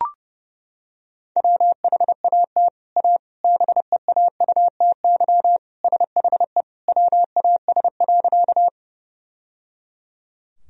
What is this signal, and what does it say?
Morse code, heard as a keyed tone, about 30 wpm, 700 Hz. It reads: WHAT A BEAUTY SHE WAS.
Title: Sentences from Top 1000 Words - Review Part 2 - 30wpm